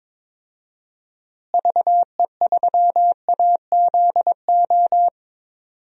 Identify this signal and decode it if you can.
Morse code, heard as a keyed tone, about 22 wpm, 700 Hz. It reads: VE3AZO
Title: DXCC-100 Call Signs - 6 Characters - Repeated 2x - 22wpm